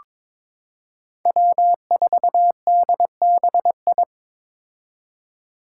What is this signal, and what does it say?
Morse code, heard as a keyed tone, about 21 wpm, 700 Hz. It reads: W4DBI
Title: Five Challenging Letter-Number Combinations 22wpm